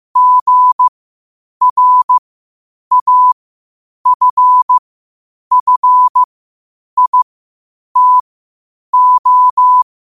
Morse code, heard as a keyed tone, about 15 wpm, spaced as for 8 wpm, 1000 Hz. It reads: GRAFFITO
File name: 8wpm